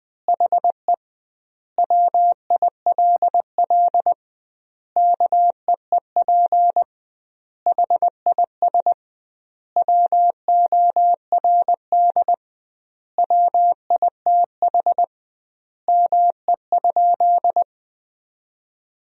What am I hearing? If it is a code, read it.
Morse code, heard as a keyed tone, about 19 wpm, 700 Hz. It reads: HE WILL KEEP HIS WORD WITH ME?
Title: 2x Word-Spacing - Sentences from Top 300 Words - 20wpm